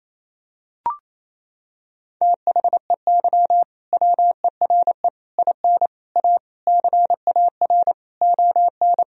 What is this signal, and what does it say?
Morse code, heard as a keyed tone, about 28 wpm, 700 Hz. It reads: THEY WERE IN A CAR ON
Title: Sentences from Top 1000 Words - Review Part 1 - 28wpm